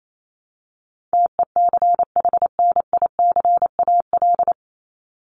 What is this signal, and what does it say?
Morse code, heard as a keyed tone, about 28 wpm, 700 Hz. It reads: TECHNICAL
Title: Top 1001-1250 Words 28wpm